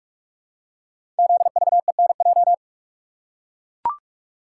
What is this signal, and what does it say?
Morse code, heard as a keyed tone, about 44 wpm, 690 Hz. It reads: ZVENJ
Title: Code Groups - 5 Letters - 45wpm